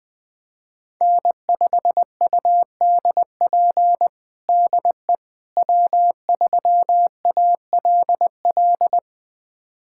Morse code, heard as a keyed tone, about 20 wpm, 700 Hz. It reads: N5UDP DE W3ALL
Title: USA Callsign DE Callsign - 5 Characters - 20wpm